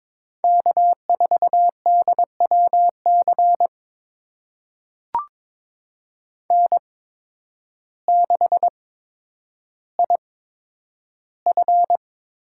Morse code, heard as a keyed tone, about 21 wpm, spaced as for 10 wpm, 700 Hz. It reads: K4DWC N6IF
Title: USA Call Signs - 5 Characters - 8x Character Spacing - 22wpm